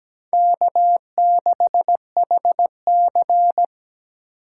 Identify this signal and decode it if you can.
Morse code, heard as a keyed tone, about 20 wpm, 700 Hz. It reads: K6HC
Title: Speed Racing - USA 1x2 Call Signs 17wpm